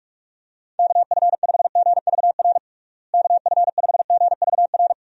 Morse code, heard as a keyed tone, about 46 wpm, 700 Hz. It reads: KF5GVR KF5GVR
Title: USA Call Signs - 6 Characters - Repeated 3x - 45wpm